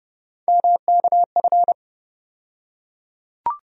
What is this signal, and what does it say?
Morse code, heard as a keyed tone, about 29 wpm, 700 Hz. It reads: MKF
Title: USA Call Sign Postfixes - 3 Letters 30wpm